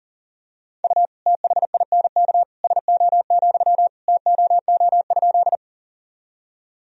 Morse code, heard as a keyed tone, about 40 wpm, 700 Hz. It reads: U THINK SO, TOO?
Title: Sentences from Top 200 Words and Abbreviations 40wpm